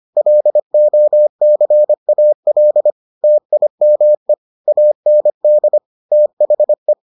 Morse code, heard as a keyed tone, about 25 wpm, 600 Hz. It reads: LOCAL TIME AND THE